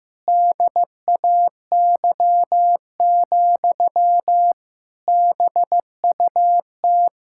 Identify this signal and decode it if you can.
Morse code, heard as a keyed tone, about 15 wpm, 700 Hz. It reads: DAY, BUT
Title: Sentences from Top 1600 Words 15wpm